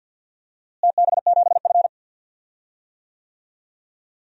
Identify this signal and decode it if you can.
Morse code, heard as a keyed tone, about 52 wpm, 700 Hz. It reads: TB7F
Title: DXCC-100 Call Signs - 4 Characters - Repeated 2x - 50wpm